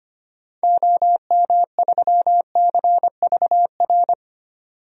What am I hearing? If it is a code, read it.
Morse code, heard as a keyed tone, about 25 wpm, 700 Hz. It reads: OM3CVR
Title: DXCC-100 Call Signs - 6 Characters - 25wpm